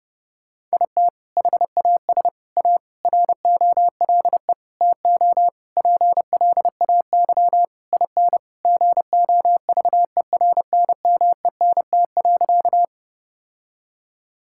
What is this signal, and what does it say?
Morse code, heard as a keyed tone, about 30 wpm, 700 Hz. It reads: IT HAS A ROLE TO PLAY IN GOVERNMENT.
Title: Sentences from Top 1000 Words - Review Part 6 - 30wpm